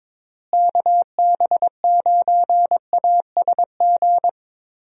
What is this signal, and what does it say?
Morse code, heard as a keyed tone, about 22 wpm, 700 Hz. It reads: KB9ASG
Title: USA Call Signs - 6 Characters - Repeated 3x - 22wpm